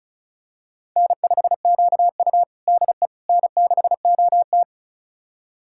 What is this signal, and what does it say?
Morse code, heard as a keyed tone, about 36 wpm, 700 Hz. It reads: N5QU DE N6OT
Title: USA Callsign DE Callsign - 4 Characters - 35wpm